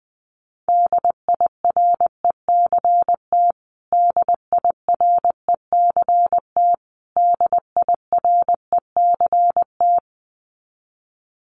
Morse code, heard as a keyed tone, about 20 wpm, 700 Hz. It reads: DIRECT DIRECT DIRECT
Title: Top 500 Words Repeated 20wpm